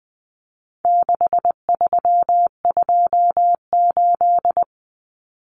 Morse code, heard as a keyed tone, about 20 wpm, 700 Hz. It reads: 6328
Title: Common Cut Numbers - 4-Digit 20wpm